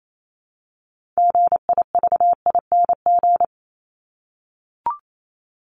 Morse code, heard as a keyed tone, about 28 wpm, 700 Hz. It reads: GIVING
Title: ING Suffix - Top 100 Words with Base Words 28wpm